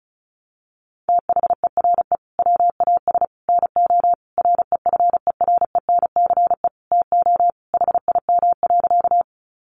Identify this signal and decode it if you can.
Morse code, heard as a keyed tone, about 34 wpm, 700 Hz. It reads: THERE WAS NO REFERENCE TO HIM.
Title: Sentences from Top 1200 Words 35wpm